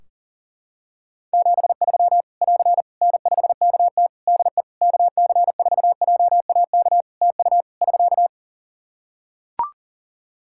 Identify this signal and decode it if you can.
Morse code, heard as a keyed tone, about 37 wpm, 700 Hz. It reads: 73 <AR> N5KT DE KC4JAK TU <SK>
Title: K1USN SST QSO Practice 40wpm